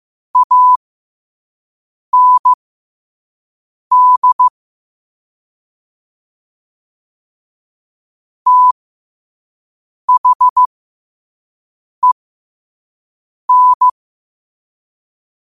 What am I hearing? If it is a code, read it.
Morse code, heard as a keyed tone, about 15 wpm, spaced as for 5 wpm, 1000 Hz. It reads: AND THEN
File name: qotdfiles / 5wpm